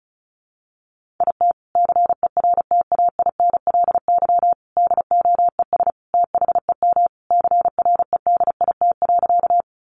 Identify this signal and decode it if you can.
Morse code, heard as a keyed tone, about 34 wpm, 700 Hz. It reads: IT CERTAINLY DOES THEM CREDIT.
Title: Sentences from Top 1000 Words 35wpm